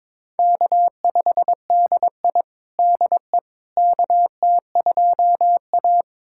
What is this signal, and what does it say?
Morse code, heard as a keyed tone, about 21 wpm, 700 Hz. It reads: K5DI DE KT2A
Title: USA Callsign DE Callsign - 4 Characters - 22wpm